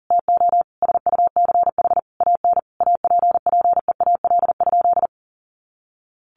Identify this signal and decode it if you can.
Morse code, heard as a keyed tone, about 40 wpm, 700 Hz. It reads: TO SUCH AN APPEAL?
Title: Sentences from Top 1300 Words 40wpm